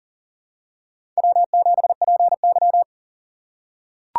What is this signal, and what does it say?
Morse code, heard as a keyed tone, about 40 wpm, 700 Hz. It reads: W7PY
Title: USA Call Signs - 4 Characters - 6x Character Spacing - 40wpm